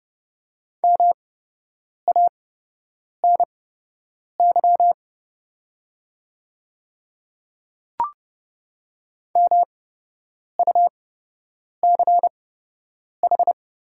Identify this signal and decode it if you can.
Morse code, heard as a keyed tone, about 30 wpm, spaced as for 8 wpm, 700 Hz. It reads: MANY MUCH
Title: 8x Character Spacing - Top 300 Words - 30wpm